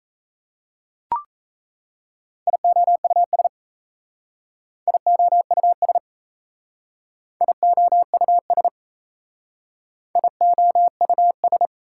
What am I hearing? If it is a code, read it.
Morse code, heard as a keyed tone, about 34 wpm, 700 Hz. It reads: IOUS IOUS IOUS IOUS
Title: Speed-Racing Prefixes and Suffixes 28wpm